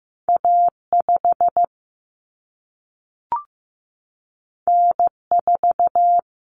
Morse code, heard as a keyed tone, about 15 wpm, 700 Hz. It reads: A5 N4
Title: Common Cut Numbers - 2-Digit 15wpm